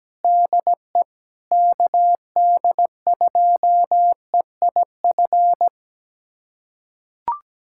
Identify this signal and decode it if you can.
Morse code, heard as a keyed tone, about 17 wpm, 700 Hz. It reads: DE KD2EIF
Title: USA Callsign DE Callsign - 6 Characters - 17wpm